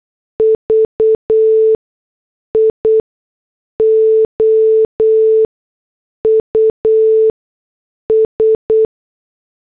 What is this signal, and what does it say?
Morse code, heard as a keyed tone, about 8 wpm, spaced as for 6 wpm, 440 Hz. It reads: VIOUS